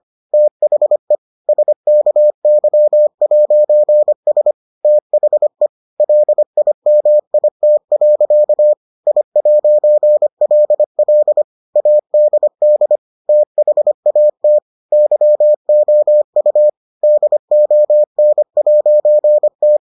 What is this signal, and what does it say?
Morse code, heard as a keyed tone, about 25 wpm, 600 Hz. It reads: THE SKY'S THE LIMIT. I'LL ADD THAT YOU DON'T